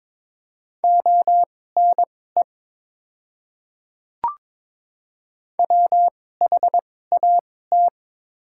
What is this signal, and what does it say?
Morse code, heard as a keyed tone, about 21 wpm, spaced as for 16 wpm, 690 Hz. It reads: ONE WHAT
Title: 2x Character Spacing - Top 100 Words - 22wpm